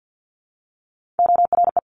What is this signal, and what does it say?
Morse code, heard as a keyed tone, about 50 wpm, 700 Hz. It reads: ORE
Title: Top Three Letter Combinations in Words 50wpm